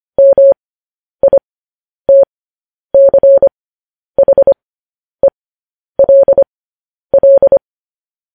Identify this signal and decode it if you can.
Morse code, heard as a keyed tone, about 25 wpm, spaced as for 10 wpm, 550 Hz. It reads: MITCHELL
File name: podcast / News headlines-1x-25x10